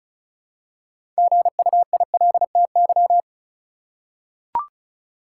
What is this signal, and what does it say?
Morse code, heard as a keyed tone, about 36 wpm, 700 Hz. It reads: GUILTY
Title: Six Letter Words - 35wpm